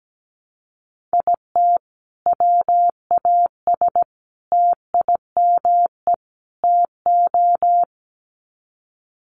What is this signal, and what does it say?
Morse code, heard as a keyed tone, about 17 wpm, 700 Hz. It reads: IT WAS TIME TO
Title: Sets of 4 Words 17wpm